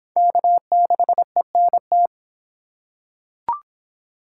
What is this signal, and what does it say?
Morse code, heard as a keyed tone, about 23 wpm, 700 Hz. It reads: K6ENT